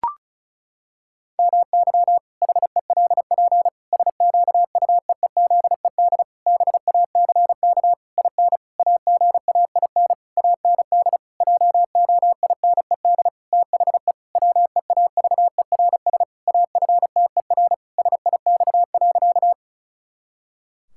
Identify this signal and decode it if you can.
Morse code, heard as a keyed tone, about 36 wpm, 700 Hz. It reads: MY HELP SQUEEZED BACK IN AGAIN AND JOINED THE WEAVERS AFTER SIX.
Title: Pangram Challenge v2 - 35wpm